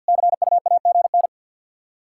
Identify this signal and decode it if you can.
Morse code, heard as a keyed tone, about 48 wpm, 700 Hz. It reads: CUAGN
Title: Practice QSO - Exchange 4 - Ending - 50wpm